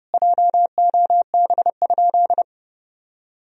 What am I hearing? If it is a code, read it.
Morse code, heard as a keyed tone, about 32 wpm, 700 Hz. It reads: JOB?
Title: Sentences from Top 1000 Words - Review Part 3 - 30wpm